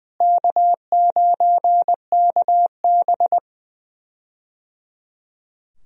Morse code, heard as a keyed tone, about 20 wpm, 700 Hz. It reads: K9KB
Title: USA Call Signs - 4 Characters - Repeated 3x - 20wpm